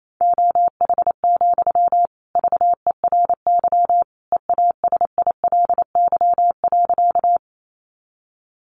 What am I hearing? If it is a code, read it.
Morse code, heard as a keyed tone, about 28 wpm, 700 Hz. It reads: OH, VERY EASILY.
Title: Sentences from Top 900 Words 28wpm